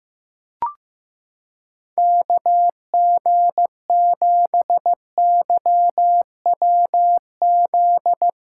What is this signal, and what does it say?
Morse code, heard as a keyed tone, about 16 wpm, 700 Hz. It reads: KG7YWZ
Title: USA Callsign DE Callsign - 6 Characters - 15wpm